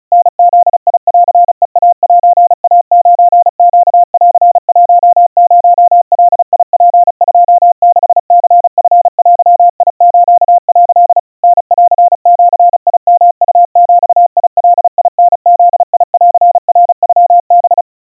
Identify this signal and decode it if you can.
Morse code (Japanese Wabun code), heard as a keyed tone, about 35 wpm, 700 Hz. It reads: タブンヘイセイ9ネン10ガツ26ニチデス」 タンジヨウビガタブンナノハ
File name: kawa / Morse sample / kamyu 130 cpm 700 Hz Crispy